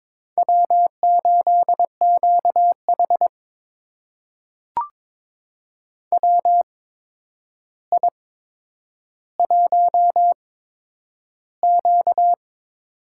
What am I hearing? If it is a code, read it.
Morse code, heard as a keyed tone, about 22 wpm, spaced as for 9 wpm, 700 Hz. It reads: W8QH WI1Q
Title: USA Call Signs - 4 Characters - 8x Character Spacing - 22wpm